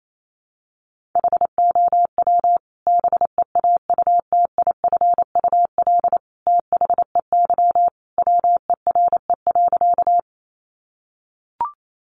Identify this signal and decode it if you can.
Morse code, heard as a keyed tone, about 28 wpm, 700 Hz. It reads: HOW BEAUTIFUL THEY WERE.
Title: Sentences from Top 900 Words 28wpm